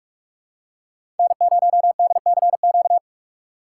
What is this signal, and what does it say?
Morse code, heard as a keyed tone, about 47 wpm, 700 Hz. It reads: N0DCQ
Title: USA Call Signs - 5 Characters - 2x Character Spacing - 45wpm